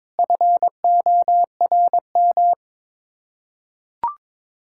Morse code, heard as a keyed tone, about 22 wpm, 700 Hz. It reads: FORM
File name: practice / 22wpm / Sets of 6 Words - 22wpm